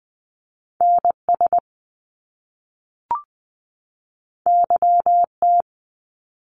Morse code, heard as a keyed tone, about 19 wpm, 700 Hz. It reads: NS YT
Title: Canadian Provinces and Territories 20wpm